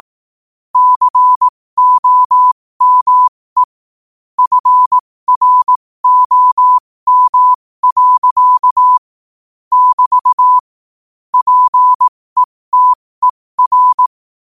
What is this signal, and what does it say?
Morse code, heard as a keyed tone, about 18 wpm, spaced as for 15 wpm, 1000 Hz. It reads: COME FROM. <BT> PETER
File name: qotdfiles / 15wpm